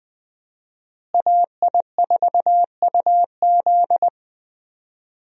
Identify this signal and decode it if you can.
Morse code, heard as a keyed tone, about 20 wpm, 700 Hz. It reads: AI4UZ
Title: USA Call Signs - 5 Characters - Repeated 3x - 20wpm